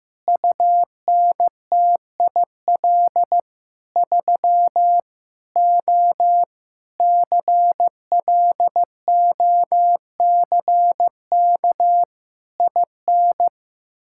Morse code, heard as a keyed tone, about 15 wpm, 700 Hz. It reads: UNTIL 3 O CLOCK IN
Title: Pangram Challenge v2 - 15wpm